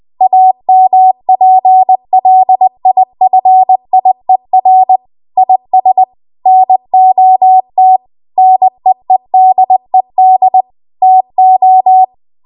Morse code, heard as a keyed tone, about 20 wpm, 750 Hz. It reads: AMPLIFIER IS NOT NEEDED TO
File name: Archive / 20 WPM / 20WPM